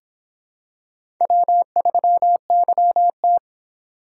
Morse code, heard as a keyed tone, about 26 wpm, 700 Hz. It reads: W3YT